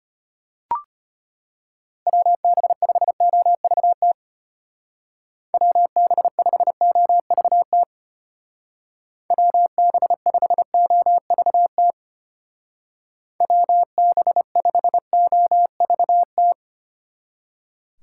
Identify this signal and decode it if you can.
Morse code, heard as a keyed tone, about 30 wpm, 700 Hz. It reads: WB5OVT WB5OVT WB5OVT WB5OVT
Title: Speed Racing - USA Call Signs 25wpm